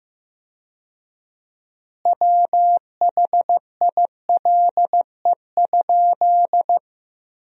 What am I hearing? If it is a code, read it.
Morse code, heard as a keyed tone, about 15 wpm, 700 Hz. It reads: WHILE?
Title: 4x Word-Spacing - Sentences from Top 300 Words - 15wpm